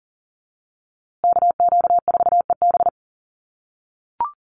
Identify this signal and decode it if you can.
Morse code, heard as a keyed tone, about 40 wpm, 700 Hz. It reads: KQ4EB
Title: USA Call Signs Repeated 3x 40wpm